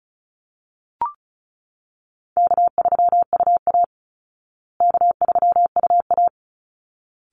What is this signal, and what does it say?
Morse code, heard as a keyed tone, about 34 wpm, 700 Hz. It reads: K3UA K3UA
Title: Easy Call Signs Repeated 2x - 35wpm